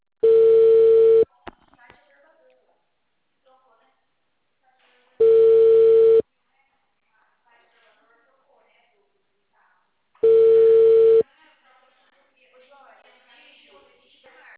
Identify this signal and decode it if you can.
Ringback tone (China, 450 Hz), with speech.
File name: out- -2009- - - .35841